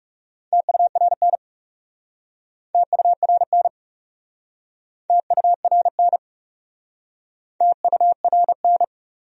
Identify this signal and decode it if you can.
Morse code, heard as a keyed tone, about 42 wpm, 700 Hz. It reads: TURN TURN TURN TURN